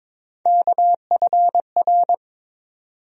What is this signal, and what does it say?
Morse code, heard as a keyed tone, about 22 wpm, 700 Hz. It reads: KFR
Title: USA Call Sign Postfixes - 3 Letters 22wpm